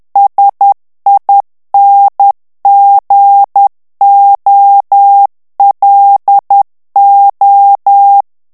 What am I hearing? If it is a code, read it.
Morse code, heard as a keyed tone, about 10 wpm, 800 Hz. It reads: SINGOLO
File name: ITALIEN texte clair 2 vitesse 10 mots minute